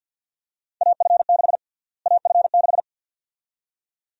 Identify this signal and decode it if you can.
Morse code, heard as a keyed tone, about 50 wpm, 700 Hz. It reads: AF6 AF6
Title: USA Call Sign Prefixes - Repeated 2x - 50wpm